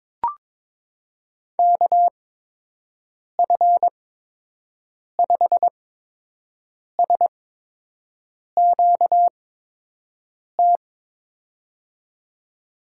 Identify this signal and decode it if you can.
Morse code, heard as a keyed tone, about 22 wpm, spaced as for 6 wpm, 700 Hz. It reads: KF5SQT
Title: USA Call Signs - 8x Character Spacing - 22wpm